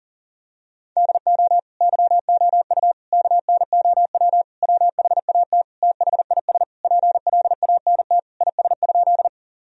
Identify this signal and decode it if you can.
Morse code, heard as a keyed tone, about 40 wpm, 700 Hz. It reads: DO YOU KNOW WHAT THIS PLANT IS?
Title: Sentences from Top 1000 Words - Review Part 1 - 40wpm